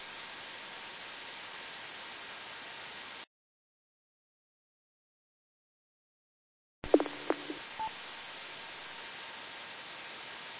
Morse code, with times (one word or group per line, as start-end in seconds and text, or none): none